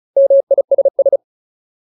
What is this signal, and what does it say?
Morse code, heard as a keyed tone, about 34 wpm, 550 Hz. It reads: MISS
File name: Ultimate Word List / 35WPM Ultimate